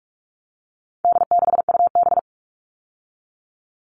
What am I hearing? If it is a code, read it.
Morse code, heard as a keyed tone, about 44 wpm, 700 Hz. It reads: D6UB
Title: All Cut Numbers - 4-Digit 45wpm